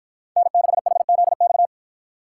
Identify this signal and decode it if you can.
Morse code, heard as a keyed tone, about 53 wpm, 700 Hz. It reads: N6HZX